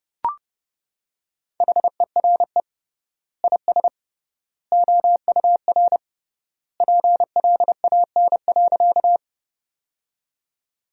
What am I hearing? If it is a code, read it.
Morse code, heard as a keyed tone, about 30 wpm, 700 Hz. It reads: HERE IS OUR PLAN.
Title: 3x Word-Spacing - Sentences from Top 500 Words - 30wpm